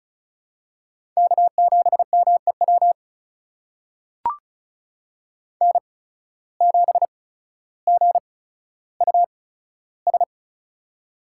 Morse code, heard as a keyed tone, about 36 wpm, spaced as for 15 wpm, 700 Hz. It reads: K7MEW N7GUS
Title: USA Call Signs - 5 Characters - 8x Character Spacing - 35wpm